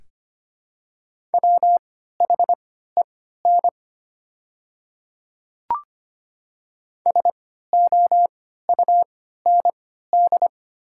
Morse code, heard as a keyed tone, about 25 wpm, spaced as for 14 wpm, 690 Hz. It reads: WHEN SOUND
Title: 3x Character Spacing - Top 200 Words - 25wpm